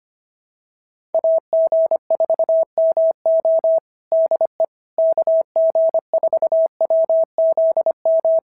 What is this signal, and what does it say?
Morse code, heard as a keyed tone, about 25 wpm, 650 Hz. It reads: AG4MO DE KG4WZM
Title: SOTA QSO Practice 25wpm